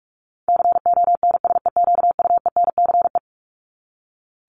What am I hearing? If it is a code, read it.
Morse code, heard as a keyed tone, about 44 wpm, 700 Hz. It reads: CONSEQUENCE
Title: LY Suffix - Top 100 Words with Base Words 45wpm